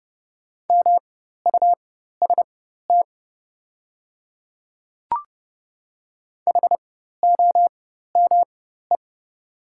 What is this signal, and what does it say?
Morse code, heard as a keyed tone, about 30 wpm, spaced as for 14 wpm, 700 Hz. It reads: MUST HOME
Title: 4x Character Spacing - Top 200 Words - 30wpm